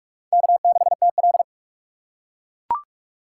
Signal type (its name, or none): Morse code